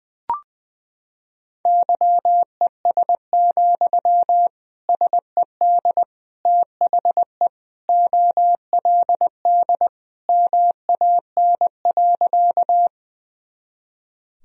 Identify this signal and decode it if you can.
Morse code, heard as a keyed tone, about 20 wpm, 700 Hz. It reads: YES, SED THE OLD MAN.
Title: Sentences from Top 500 Words and Abbreviations 20wpm